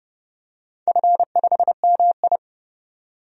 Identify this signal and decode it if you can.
Morse code, heard as a keyed tone, about 30 wpm, 700 Hz. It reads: F5MI
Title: DXCC-100 Call Signs - 4 Characters - 30wpm